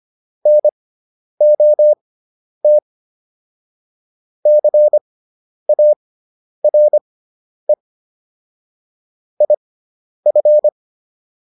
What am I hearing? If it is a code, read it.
Morse code, heard as a keyed tone, about 25 wpm, spaced as for 10 wpm, 600 Hz. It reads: NOT CARE IF